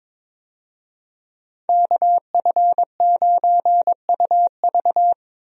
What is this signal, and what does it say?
Morse code, heard as a keyed tone, about 22 wpm, 700 Hz. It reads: KF9UV
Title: USA Call Signs - 5 Characters - Repeated 3x - 22wpm